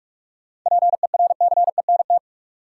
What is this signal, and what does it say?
Morse code, heard as a keyed tone, about 46 wpm, 700 Hz. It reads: PERCENT
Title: Sets of 2 Words - New Word Drill - 45wpm